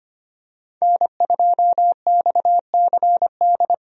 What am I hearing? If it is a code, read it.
Morse code, heard as a keyed tone, about 25 wpm, 700 Hz. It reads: N2XCD
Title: USA Call Signs - 5 Characters - 25wpm